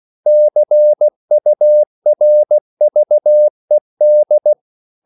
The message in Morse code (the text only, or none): CURVED